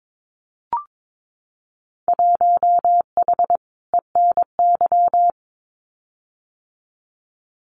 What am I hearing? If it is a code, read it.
Morse code, heard as a keyed tone, about 22 wpm, 700 Hz. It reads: 1H ENY
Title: Winter Field Day Exchanges 22wpm